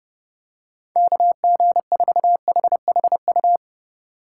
Morse code, heard as a keyed tone, about 29 wpm, 700 Hz. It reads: KG4HHU